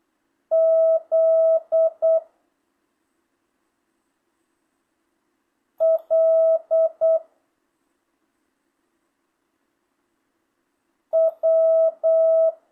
Morse code, heard as a keyed tone, about 8 wpm, 660 Hz. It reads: Z L W
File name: jh8cbh / 2025cw